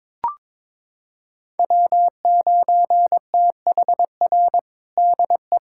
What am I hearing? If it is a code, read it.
Morse code, heard as a keyed tone, about 22 wpm, 700 Hz. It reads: W9THR DE
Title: USA Callsign DE Callsign - 5 Characters - 22wpm